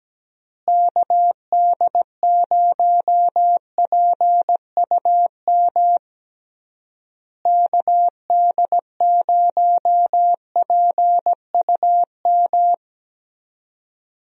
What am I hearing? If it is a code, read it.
Morse code, heard as a keyed tone, about 17 wpm, 700 Hz. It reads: KD0PUM KD0PUM
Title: USA Call Signs - 6 Characters - Repeated 3x - 17wpm